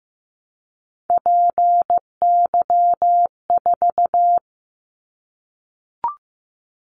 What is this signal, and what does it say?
Morse code, heard as a keyed tone, about 15 wpm, 700 Hz. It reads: PY4
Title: Callsign Prefixes for Top 100 DXCC - 15wpm